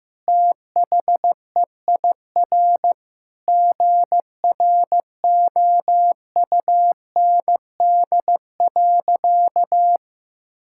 Morse code, heard as a keyed tone, about 16 wpm, 700 Hz. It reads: THEIR GROUND.
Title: Sentences from Top 1000 Words - Review Part 5 - 15wpm